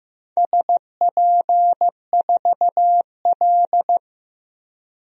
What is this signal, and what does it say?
Morse code, heard as a keyed tone, about 15 wpm, 700 Hz. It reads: SP4L
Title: DXCC-100 Call Signs - 4 Characters - Repeated 3x - 15wpm